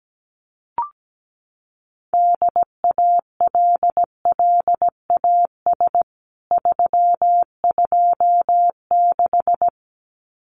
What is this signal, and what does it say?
Morse code, heard as a keyed tone, about 17 wpm, 700 Hz. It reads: DALLAS 326
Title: CWT Exchange 17wpm